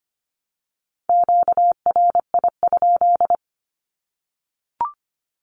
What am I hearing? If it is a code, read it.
Morse code, heard as a keyed tone, about 25 wpm, 700 Hz. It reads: QRI?
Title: Q-Codes Repeated 3x 25wpm